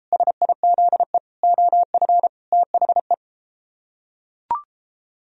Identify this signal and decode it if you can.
Morse code, heard as a keyed tone, about 33 wpm, 700 Hz. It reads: SIZE OF THE